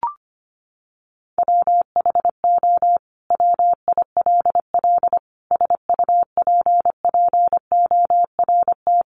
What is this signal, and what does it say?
Morse code, heard as a keyed tone, about 25 wpm, 700 Hz. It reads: WHO WILL SUPPORT
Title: Sentences from Top 800 Words 25wpm